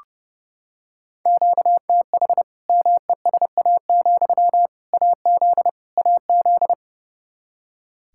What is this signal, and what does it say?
Morse code, heard as a keyed tone, about 30 wpm, 700 Hz. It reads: QTH MESA, AZ AZ
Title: Practice QSO - Exchange 1 - Beginning - 30wpm